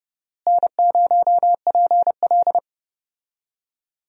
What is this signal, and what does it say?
Morse code, heard as a keyed tone, about 30 wpm, 700 Hz. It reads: N0PL
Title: USA Call Signs - 4 Characters - Repeated 2x - 30wpm